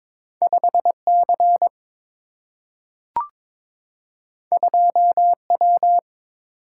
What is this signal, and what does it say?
Morse code, heard as a keyed tone, about 22 wpm, 700 Hz. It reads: 5C 2W
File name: practice / 22wpm / Two Letter-Number Combinations - 22wpm